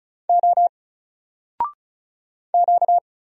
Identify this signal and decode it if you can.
Morse code, heard as a keyed tone, about 35 wpm, 700 Hz. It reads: O Q
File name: Single-Letter-Number-35wpm